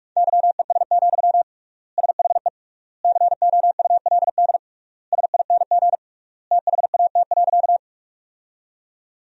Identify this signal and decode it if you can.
Morse code, heard as a keyed tone, about 46 wpm, 700 Hz. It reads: YES, SHE COULD SING THAT.
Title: 3x Word-Spacing - Sentences from Top 500 Words - 45wpm